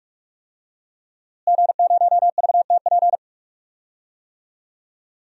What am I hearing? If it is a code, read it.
Morse code, heard as a keyed tone, about 41 wpm, 700 Hz. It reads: G0VTP
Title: DXCC-100 Call Signs - 5 Characters - Repeated 3x - 45wpm